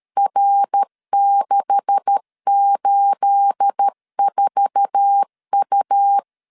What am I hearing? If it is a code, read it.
Morse code, heard as a keyed tone, about 13 wpm, 800 Hz. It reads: R684U
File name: LEZIONI / 10 SIGNS / 10A SLASH 12wpm 2314